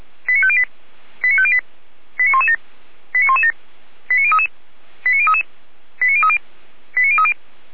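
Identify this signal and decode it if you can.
Five-tone selective call (ZVEI), heard as quick runs of five tones, tones of about 70 ms. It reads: 89508 89508 89117 89117 89033 89033 89030 89030
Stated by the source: ZVEI five-tone selective call received by SDR.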